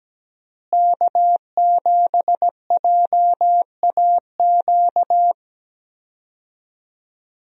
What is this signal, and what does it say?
Morse code, heard as a keyed tone, about 17 wpm, 700 Hz. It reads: K7JAQ
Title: USA Call Signs - 5 Characters - Repeated 2x - 17wpm